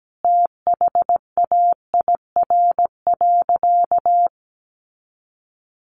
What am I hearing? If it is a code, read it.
Morse code, heard as a keyed tone, about 17 wpm, 700 Hz. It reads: THAIR.
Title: A Sound Of Thunder 17wpm